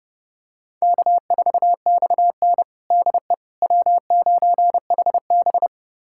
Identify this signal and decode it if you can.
Morse code, heard as a keyed tone, about 32 wpm, 700 Hz. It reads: K4XN DE W9HB
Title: USA Callsign DE Callsign - 4 Characters - 30wpm